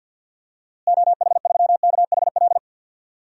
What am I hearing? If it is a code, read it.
Morse code, heard as a keyed tone, about 46 wpm, 700 Hz. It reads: OH3KHL
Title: DXCC-100 Call Signs - 6 Characters - Repeated 3x - 50wpm